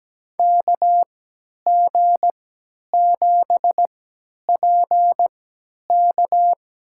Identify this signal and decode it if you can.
Morse code, heard as a keyed tone, about 17 wpm, spaced as for 10 wpm, 700 Hz. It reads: KG7PK
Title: USA Call Signs - 3x Character Spacing - 17wpm